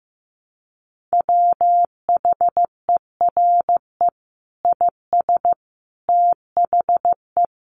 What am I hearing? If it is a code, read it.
Morse code, heard as a keyed tone, about 15 wpm, 700 Hz. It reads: WHERE IS THE
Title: Sentences from Top 1100 Words 15wpm